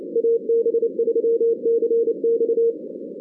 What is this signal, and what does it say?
Morse code, heard as a keyed tone, about 27 wpm, 470 Hz. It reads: AB3CX
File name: 2016 arrl dx cw / 2016-ARRL-DX-CW-K5TR- - 0-22